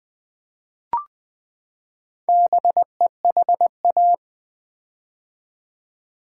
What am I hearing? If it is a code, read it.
Morse code, heard as a keyed tone, about 21 wpm, 700 Hz. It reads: BEHA
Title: Build A Word - 4 Letters At A Time - 10-Letter Words - 20wpm